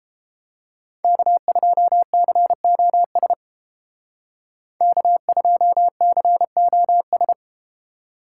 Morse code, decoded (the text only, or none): K2COS K2COS